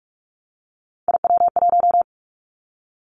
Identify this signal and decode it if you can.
Morse code, heard as a keyed tone, about 46 wpm, 700 Hz. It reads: IW1
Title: Callsign Prefixes for Top 25 DXCC - 45wpm